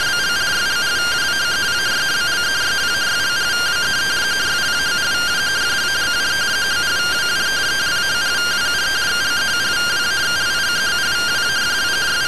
Two-tone teleprinter signal, about 170 Hz shift.